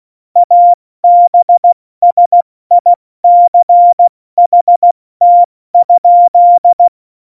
Morse code, heard as a keyed tone, about 16 wpm, spaced as for 14 wpm, 700 Hz. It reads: ABSICHT?
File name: Morsekurs / Witze / Witz 24. Oktober 17, Tempo 70